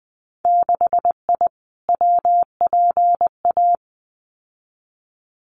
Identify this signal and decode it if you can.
Morse code, heard as a keyed tone, about 20 wpm, 700 Hz. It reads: 6I WPA
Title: Winter Field Day Exchanges Encore 20wpm